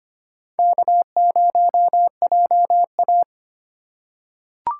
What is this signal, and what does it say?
Morse code, heard as a keyed tone, about 25 wpm, 700 Hz. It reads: K0JA
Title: USA Call Signs - 4 Characters - 3x Character Spacing - 25wpm